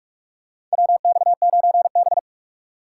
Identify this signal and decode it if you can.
Morse code, heard as a keyed tone, about 46 wpm, 690 Hz. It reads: WX9B
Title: USA Call Signs - 4 Characters - 2x Character Spacing - 45wpm